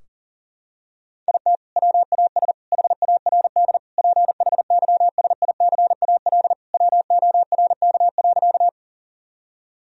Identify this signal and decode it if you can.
Morse code, heard as a keyed tone, about 40 wpm, 700 Hz. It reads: IT WAS HARD PHYSICAL WORK.
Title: Sentences from Top 1000 Words - Review Part 4 - 40wpm